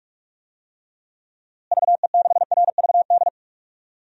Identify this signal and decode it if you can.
Morse code, heard as a keyed tone, about 44 wpm, 700 Hz. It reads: VE6RVD
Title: DXCC-100 Call Signs - 6 Characters - Repeated 3x - 45wpm